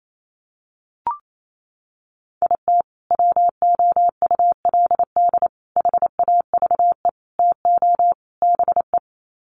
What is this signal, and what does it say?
Morse code, heard as a keyed tone, about 28 wpm, 700 Hz. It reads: IT WOULD HAVE TO BE
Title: Sets of 5 Words Encore 28wpm